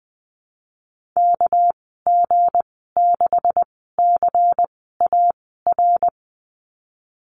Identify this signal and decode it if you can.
Morse code, heard as a keyed tone, about 20 wpm, spaced as for 14 wpm, 700 Hz. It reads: KG6CAR
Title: 213 Lesson 34g - Number 8 - Call Signs 20wpm